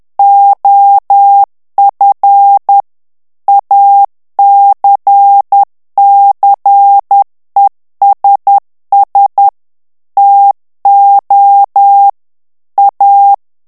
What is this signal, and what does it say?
Morse code, heard as a keyed tone, about 10 wpm, 800 Hz. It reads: OF ACCESS TO A
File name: ANGLAIS texte clair 3 vitesse 10 mots minute